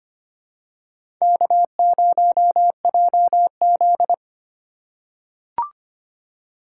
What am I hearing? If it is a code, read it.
Morse code, heard as a keyed tone, about 25 wpm, 700 Hz. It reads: K0JZ
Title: USA Call Signs - 4x Character Spacing - 25wpm